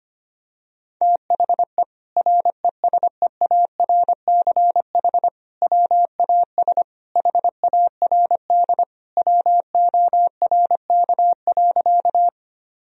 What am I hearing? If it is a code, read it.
Morse code, heard as a keyed tone, about 25 wpm, 700 Hz. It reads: THE RESEARCH WAS HARD WORK.
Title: Sentences from Top 1000 Words - Review Part 4 - 25wpm